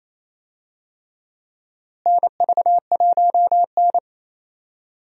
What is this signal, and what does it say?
Morse code, heard as a keyed tone, about 28 wpm, 700 Hz. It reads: NV1N
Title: USA Call Signs - 4 Characters - Repeated 3x - 28wpm